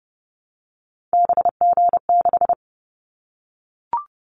Morse code, heard as a keyed tone, about 31 wpm, 700 Hz. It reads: BG6
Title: Callsign Prefixes for Top 100 DXCC - 30wpm